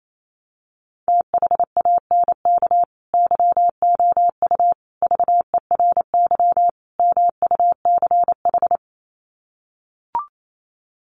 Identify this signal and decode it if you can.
Morse code, heard as a keyed tone, about 28 wpm, 700 Hz. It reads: THANK YOU VERY MUCH
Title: Sets of 4 Words 28wpm